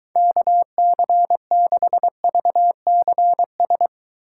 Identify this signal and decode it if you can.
Morse code, heard as a keyed tone, about 24 wpm, 700 Hz. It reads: KC6VCS